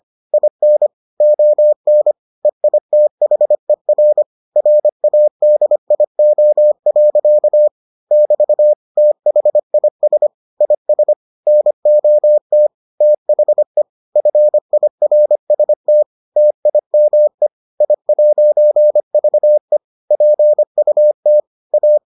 Morse code, heard as a keyed tone, about 25 wpm, 600 Hz. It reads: IN ON EITHER RADIO. <BT> THIS IS NOT THE FIRST TIME I'VE PUT A